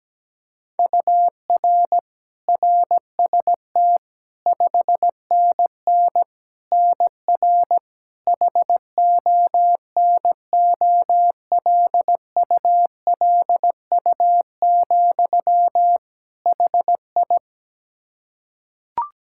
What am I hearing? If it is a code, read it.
Morse code, heard as a keyed tone, about 17 wpm, 700 Hz. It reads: UR RST 5NN NR HONOLULU, HI
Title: Practice QSO - Exchange 1 - Beginning - 17wpm